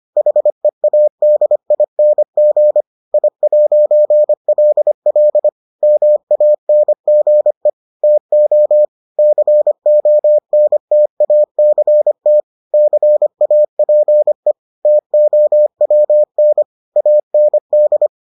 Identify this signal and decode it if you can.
Morse code, heard as a keyed tone, about 25 wpm, 600 Hz. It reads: HEADING I'LL MANGE TO CONTACT CAPE TOWN AND